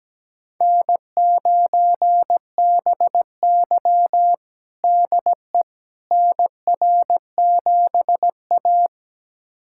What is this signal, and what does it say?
Morse code, heard as a keyed tone, about 17 wpm, 700 Hz. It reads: N9BY DE NR7A
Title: USA Callsign DE Callsign - 4 Characters - 17wpm